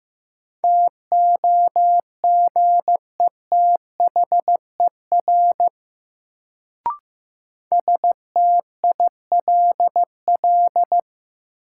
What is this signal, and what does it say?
Morse code, heard as a keyed tone, about 15 wpm, 700 Hz. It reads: TOGETHER STILL
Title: Sentences from Top 300 Words - New Word Drill - 15wpm